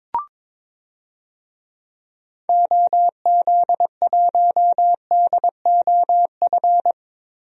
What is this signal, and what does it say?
Morse code, heard as a keyed tone, about 22 wpm, 700 Hz. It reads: OZ1DOF
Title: DXCC-100 Call Signs - 6 Characters - Repeated 2x - 22wpm